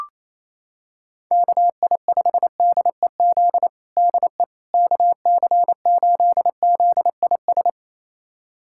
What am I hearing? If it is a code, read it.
Morse code, heard as a keyed tone, about 28 wpm, 700 Hz. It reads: KI5DEZ DE KC8ZIS
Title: USA Callsign DE Callsign 28wpm